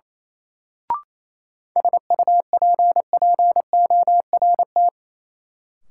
Morse code, heard as a keyed tone, about 28 wpm, 700 Hz. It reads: SUPPORT
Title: Sentences from Top 800 Words - New Word Drill - 28wpm